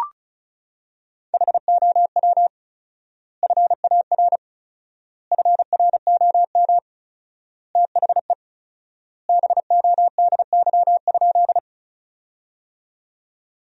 Morse code, heard as a keyed tone, about 34 wpm, 700 Hz. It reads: HOW FAR FROM THE BODY?
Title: 4x Word-Spacing - Sentences from Top 400 Words - 35wpm